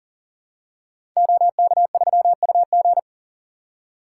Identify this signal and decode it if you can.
Morse code, heard as a keyed tone, about 39 wpm, 700 Hz. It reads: OK3UG
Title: Code Groups - 5 Letter:Numbers - 40wpm